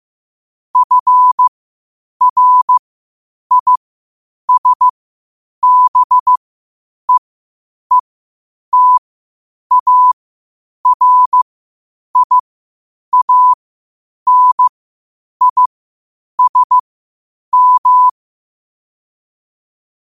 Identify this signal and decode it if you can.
Morse code, heard as a keyed tone, about 15 wpm, spaced as for 8 wpm, 1000 Hz. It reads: FRISBEETARIANISM
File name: qotdfiles / 8wpm